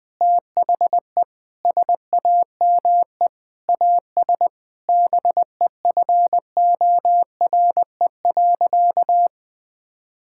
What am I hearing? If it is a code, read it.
Morse code, heard as a keyed tone, about 20 wpm, 700 Hz. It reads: THE SAME AS BEFORE.
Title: Sentences from Top 1000 Words - Review Part 5 - 20wpm